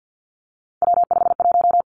Morse code, heard as a keyed tone, about 50 wpm, 700 Hz. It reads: W51
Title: Three Challenging Letter-Number Combinations 50wpm